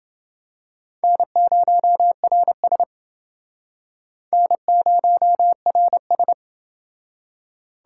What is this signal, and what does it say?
Morse code, heard as a keyed tone, about 29 wpm, 700 Hz. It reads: N0RS N0RS